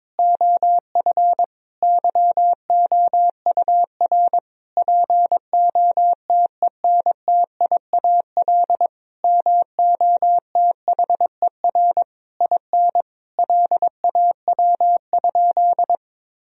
Morse code, heard as a keyed tone, about 22 wpm, 700 Hz. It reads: OF YOUR POTENTIAL MOTHER IN LAW?
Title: Sentences from Top 1000 Words - Review Part 4 - 22wpm